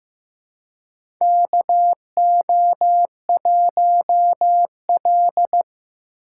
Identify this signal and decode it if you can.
Morse code, heard as a keyed tone, about 15 wpm, 700 Hz. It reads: KO1L
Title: USA Call Signs - 4 Characters - 2x Character Spacing - 15wpm